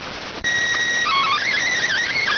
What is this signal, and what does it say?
SSTV (slow-scan television), PD 290.